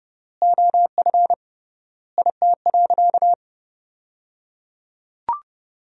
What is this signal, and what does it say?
Morse code, heard as a keyed tone, about 30 wpm, 700 Hz. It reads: OF IT.
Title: 3x Word-Spacing - Sentences from Top 400 Words - 30wpm